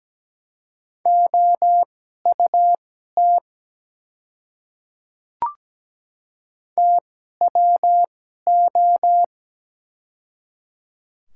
Morse code, heard as a keyed tone, about 17 wpm, spaced as for 12 wpm, 700 Hz. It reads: OUT TWO
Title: 2x Character Spacing - Top 100 Words - 17wpm